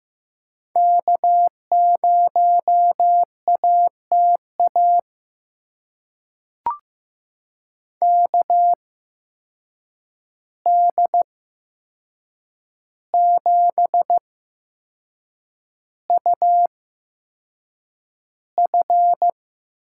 Morse code, heard as a keyed tone, about 15 wpm, spaced as for 6 wpm, 700 Hz. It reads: K0ATA KD7UF
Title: USA Call Signs - 8x Character Spacing - 15wpm